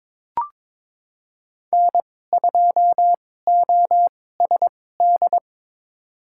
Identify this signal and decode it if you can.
Morse code, heard as a keyed tone, about 22 wpm, spaced as for 16 wpm, 700 Hz. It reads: N2OSD
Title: USA Call Signs - 5 Characters - 2x Character Spacing - 22wpm